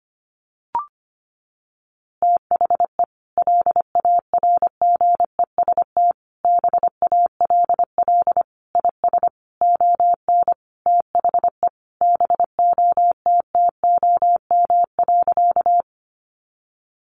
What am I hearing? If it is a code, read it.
Morse code, heard as a keyed tone, about 25 wpm, 700 Hz. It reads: THE LARGEST BALL IS ON THE BOTTOM.
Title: Easy Phrases 25wpm